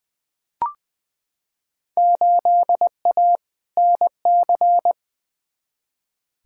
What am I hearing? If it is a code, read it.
Morse code, heard as a keyed tone, about 20 wpm, 700 Hz. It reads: 8A NC
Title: Field Day Exchanges - 20wpm